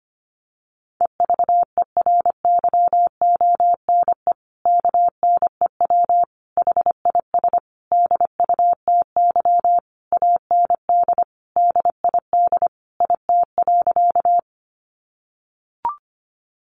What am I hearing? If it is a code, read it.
Morse code, heard as a keyed tone, about 25 wpm, 700 Hz. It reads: EVERYONE KNEW HIS DUTY AND DID IT.
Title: Sentences from Top 1500 Words 25wpm